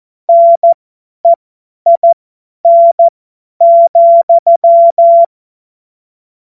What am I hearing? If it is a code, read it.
Morse code, heard as a keyed tone, about 14 wpm, spaced as for 10 wpm, 680 Hz. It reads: NEIN,
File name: Morsekurs / Witze / Witz 04. Oktober 2022 50 F70